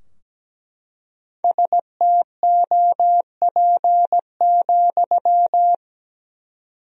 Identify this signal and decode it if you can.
Morse code, heard as a keyed tone, about 17 wpm, 700 Hz. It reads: STOP,
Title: 4x Word-Spacing - Sentences from Top 300 Words - 17wpm